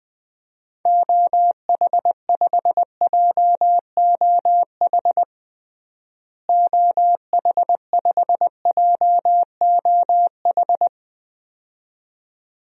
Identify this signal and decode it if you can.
Morse code, heard as a keyed tone, about 21 wpm, 690 Hz. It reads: OH5JOH OH5JOH
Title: DXCC-100 Call Signs - 6 Characters - Repeated 2x - 20wpm